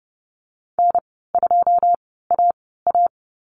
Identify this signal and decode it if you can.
Morse code, heard as a keyed tone, about 30 wpm, spaced as for 17 wpm, 700 Hz. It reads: N2AA
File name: Lesson 14g - Number 2 - Call Signs 30wpm